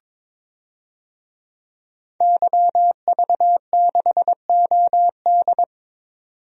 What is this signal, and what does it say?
Morse code, heard as a keyed tone, about 22 wpm, 700 Hz. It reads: YV6OD
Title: DXCC-100 Call Signs - 5 Characters - 22wpm